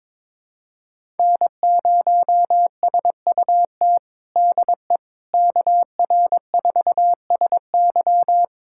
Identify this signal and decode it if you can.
Morse code, heard as a keyed tone, about 22 wpm, 700 Hz. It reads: N0SUT DE KR4SY
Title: USA Callsign DE Callsign - 5 Characters - 22wpm